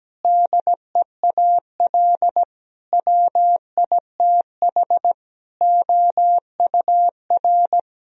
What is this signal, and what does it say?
Morse code, heard as a keyed tone, about 17 wpm, 700 Hz. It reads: DEAL WITH OUR
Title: Sentences from Top 1000 Words - Review Part 4 - 17wpm